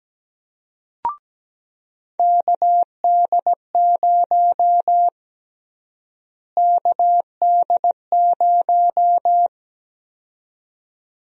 Morse code, heard as a keyed tone, about 17 wpm, 700 Hz. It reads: KD0 KD0
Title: USA Call Sign Prefixes - Repeated 2x - 17wpm